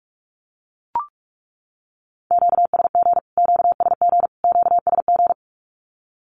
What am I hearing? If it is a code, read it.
Morse code, heard as a keyed tone, about 44 wpm, 700 Hz. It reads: QSG QSG QSG
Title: Q-Codes Repeated 3x 45wpm